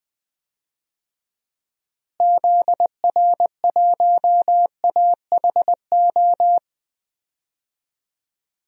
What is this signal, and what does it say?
Morse code, heard as a keyed tone, about 20 wpm, 700 Hz. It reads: ZR1AHO
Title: DXCC-100 Call Signs - 6 Characters - Repeated 2x - 20wpm